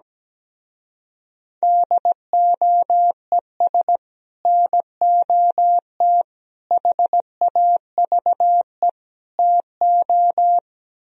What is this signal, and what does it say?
Morse code, heard as a keyed tone, about 17 wpm, 700 Hz. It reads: DOES NOT HAVE TO
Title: Speed-Racing - Sets of 4 Words 17wpm